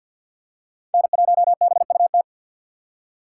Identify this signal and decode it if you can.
Morse code, heard as a keyed tone, about 47 wpm, 700 Hz. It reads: N1BUT
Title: USA Call Signs - 5 Characters - 6x Character Spacing - 50wpm